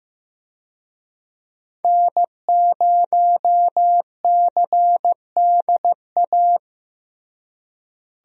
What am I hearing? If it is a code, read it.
Morse code, heard as a keyed tone, about 15 wpm, 700 Hz. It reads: N0CDA
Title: USA Call Signs - 5 Characters - Repeated 3x - 15wpm